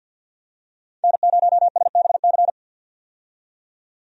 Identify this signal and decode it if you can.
Morse code, heard as a keyed tone, about 50 wpm, 700 Hz. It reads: N0SBC
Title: USA Call Signs - 5 Characters - 50wpm